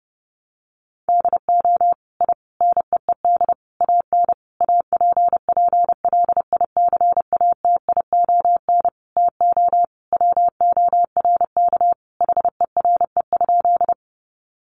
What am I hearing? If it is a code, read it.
Morse code, heard as a keyed tone, about 30 wpm, 700 Hz. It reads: DO I NEED AN APPLICATION TO WORK HERE?
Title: Sentences from Top 1200 Words 30wpm